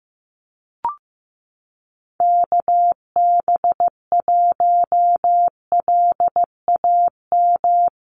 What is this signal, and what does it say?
Morse code, heard as a keyed tone, about 15 wpm, 700 Hz. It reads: KB1LAM
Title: USA Call Signs Repeated 3x 15wpm